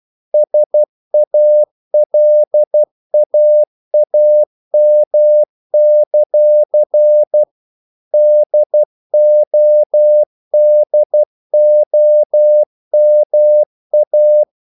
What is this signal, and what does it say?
Morse code, heard as a keyed tone, about 12 wpm, 600 Hz. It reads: SALAAM; DODOMA
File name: CW / Practice / NationalCapitals12